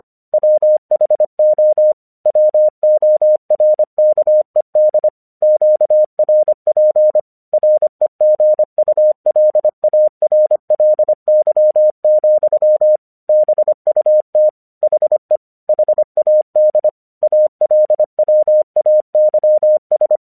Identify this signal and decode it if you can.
Morse code, heard as a keyed tone, about 25 wpm, 610 Hz. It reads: WHO WORKED QRP REGULARLY, BUT HE HAD ALWAYS
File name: 1 English 25wpm / AQC32